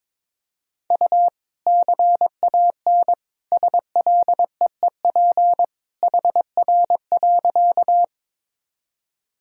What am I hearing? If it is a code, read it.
Morse code, heard as a keyed tone, about 22 wpm, 700 Hz. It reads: U CAN SLEEP HR.
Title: Sentences from Top 500 Words and Abbreviations 22wpm